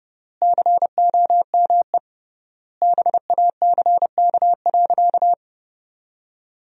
Morse code, heard as a keyed tone, about 30 wpm, 700 Hz. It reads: COME BACK.
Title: 3x Word-Spacing - Sentences from Top 300 Words - 30wpm